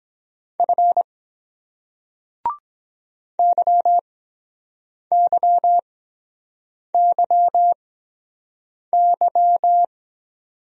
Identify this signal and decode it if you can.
Morse code, heard as a keyed tone, about 23 wpm, spaced as for 7 wpm, 700 Hz. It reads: F YYYY